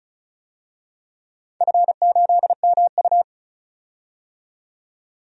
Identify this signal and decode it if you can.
Morse code, heard as a keyed tone, about 36 wpm, 700 Hz. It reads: F8MU
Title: DXCC-100 Call Signs - 4 Characters - 35wpm